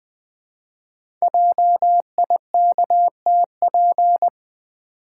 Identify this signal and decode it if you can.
Morse code, heard as a keyed tone, about 20 wpm, 700 Hz. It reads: JIKTP
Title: Code Groups - 5 Letters - 20wpm